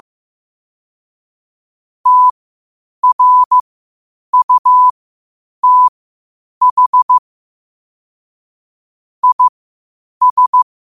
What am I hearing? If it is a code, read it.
Morse code, heard as a keyed tone, about 15 wpm, spaced as for 8 wpm, 1000 Hz. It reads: TRUTH IS
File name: qotdfiles / 8wpm